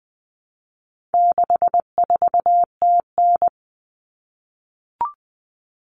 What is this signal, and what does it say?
Morse code, heard as a keyed tone, about 20 wpm, 700 Hz. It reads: 64TN
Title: Common Cut Numbers - 4-Digit 20wpm